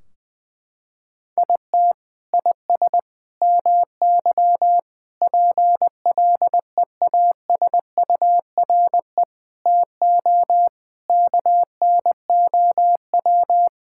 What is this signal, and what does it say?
Morse code, heard as a keyed tone, about 20 wpm, 700 Hz. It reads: IT IS MY PLEASURE TO KNOW
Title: Sentences from Top 1600 Words 20wpm